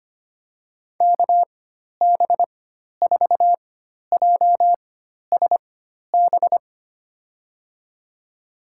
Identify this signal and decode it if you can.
Morse code, heard as a keyed tone, about 25 wpm, spaced as for 12 wpm, 700 Hz. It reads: KB4JSB
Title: USA Call Signs - 6 Characters - 4x Character Spacing - 25wpm